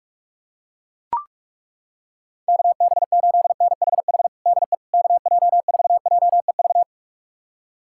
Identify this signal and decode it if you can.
Morse code, heard as a keyed tone, about 41 wpm, 700 Hz. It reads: KB8NHH DE KJ4JEV
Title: USA Callsign DE Callsign 45wpm